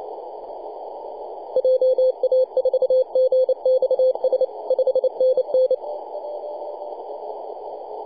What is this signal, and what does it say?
Morse code, heard as a keyed tone, about 29 wpm, 520 Hz. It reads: JA4GXS 5NN
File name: posts / 2015-june-21-0848z-yb4ir-9-14-010mhz-cw